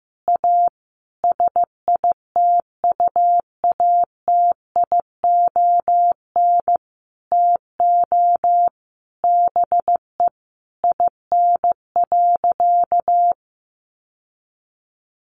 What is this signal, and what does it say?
Morse code, heard as a keyed tone, about 14 wpm, 700 Hz. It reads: A SITUATION TO BE IN.
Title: Sentences from Top 1200 Words 15wpm